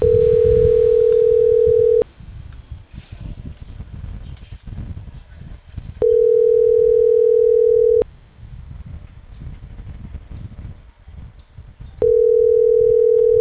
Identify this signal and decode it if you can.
Ringback tone (North America), 440 + 480 Hz.